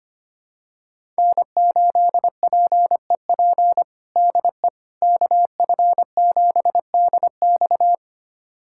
Morse code, heard as a keyed tone, about 25 wpm, 700 Hz. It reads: N8PEP DE KF7DX
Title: USA Callsign DE Callsign - 5 Characters - 25wpm